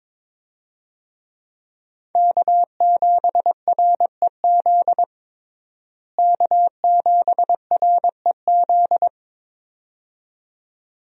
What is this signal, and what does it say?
Morse code, heard as a keyed tone, about 22 wpm, 700 Hz. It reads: K7REZ K7REZ
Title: USA Call Signs - 5 Characters - Repeated 2x - 22wpm